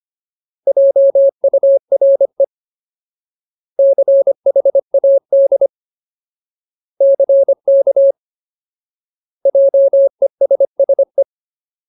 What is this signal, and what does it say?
Morse code, heard as a keyed tone, about 25 wpm, 550 Hz. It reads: JURE CHAD CK JESSE
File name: Ultimate Word List / 25WPM Ultimate